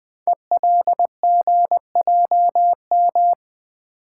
Morse code, heard as a keyed tone, about 20 wpm, 700 Hz. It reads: ELGJM
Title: Code Groups - 5 Letters - 20wpm